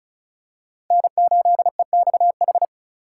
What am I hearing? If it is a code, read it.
Morse code, heard as a keyed tone, about 34 wpm, 700 Hz. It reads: N8EXH
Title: USA Call Signs - 5 Characters - 4x Character Spacing - 35wpm